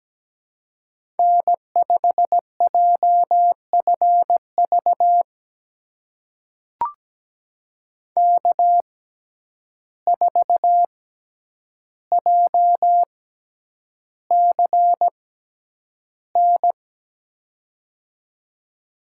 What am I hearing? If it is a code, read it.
Morse code, heard as a keyed tone, about 17 wpm, spaced as for 9 wpm, 700 Hz. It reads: N5JFV K4JCN
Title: USA Call Signs - 5 Characters - 6x Character Spacing - 17wpm